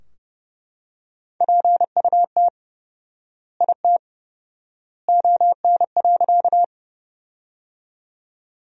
Morse code, heard as a keyed tone, about 32 wpm, 700 Hz. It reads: PUT IT ON.
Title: 4x Word-Spacing - Sentences from Top 200 Words - 30wpm